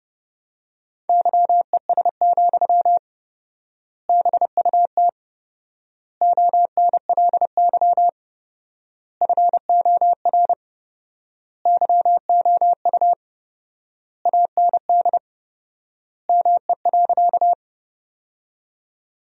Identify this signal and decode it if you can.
Morse code, heard as a keyed tone, about 30 wpm, 700 Hz. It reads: YES, BUT ONLY FOR YOU AND ME.
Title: 4x Word-Spacing - Sentences from Top 500 Words - 30wpm